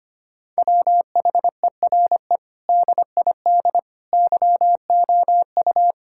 Morse code, heard as a keyed tone, about 25 wpm, 700 Hz. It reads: WHERE DID YOU
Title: Sentences from Top 1600 Words 25wpm